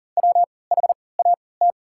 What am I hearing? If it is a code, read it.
Morse code, heard as a keyed tone, about 40 wpm, spaced as for 23 wpm, 700 Hz. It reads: WHAT
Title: 3x Character Spacing - Top 200 Words - 40wpm